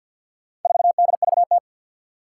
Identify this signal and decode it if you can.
Morse code, heard as a keyed tone, about 52 wpm, 700 Hz. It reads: 4DVT
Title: Four Challenging Letter-Number Combinations 50wpm